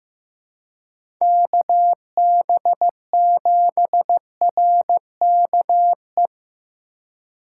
Morse code, heard as a keyed tone, about 15 wpm, 700 Hz. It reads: KB7RKE
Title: USA Call Signs - 6 Characters - 3x Character Spacing - 15wpm